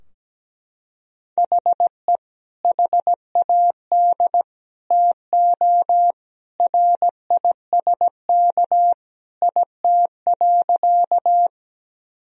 Morse code, heard as a keyed tone, about 17 wpm, 700 Hz. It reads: HE HAD TO RISK IT.
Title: Sentences from Top 1000 Words - Review Part 4 - 17wpm